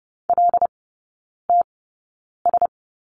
Morse code, heard as a keyed tone, about 30 wpm, spaced as for 9 wpm, 700 Hz. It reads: LTS
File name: Lesson 17e - Letter u - Three Characters 30wpm